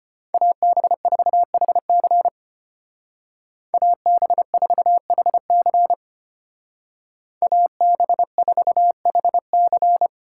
Morse code, heard as a keyed tone, about 31 wpm, 700 Hz. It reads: AB4HC AB4HC AB4HC
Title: Speed Racing - USA 2x2 Call Signs 25wpm